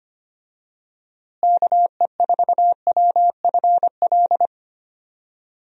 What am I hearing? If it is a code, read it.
Morse code, heard as a keyed tone, about 25 wpm, 700 Hz. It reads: KE4WFL
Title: USA Call Signs - 6 Characters - Repeated 3x - 25wpm